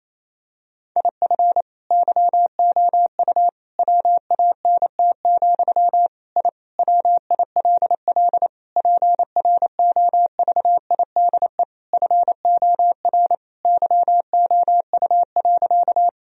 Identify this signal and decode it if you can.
Morse code, heard as a keyed tone, about 28 wpm, 700 Hz. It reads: IF YOU WANT, I WILL PROVIDE FOR YOU.
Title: Sentences from Top 1000 Words - Review Part 3 - 28wpm